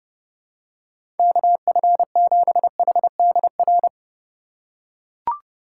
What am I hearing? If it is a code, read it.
Morse code, heard as a keyed tone, about 31 wpm, 700 Hz. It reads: KF7HDR
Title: USA Call Signs - 6 Characters - 3x Character Spacing - 30wpm